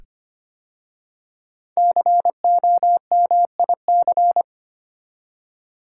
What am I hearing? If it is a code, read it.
Morse code, heard as a keyed tone, about 25 wpm, 700 Hz. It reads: COMIC
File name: 5 Letter Words - First 3 Letters Given - 25wpm